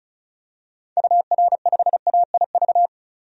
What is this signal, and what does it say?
Morse code, heard as a keyed tone, about 36 wpm, 700 Hz. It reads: UR5AIV
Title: DXCC-100 Call Signs - 6 Characters - 35wpm